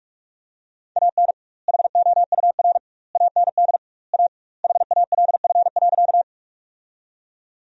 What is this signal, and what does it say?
Morse code, heard as a keyed tone, about 46 wpm, 700 Hz. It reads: AN HOUR AND A HALF.
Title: 2x Word-Spacing - Sentences from Top 400 Words - 45wpm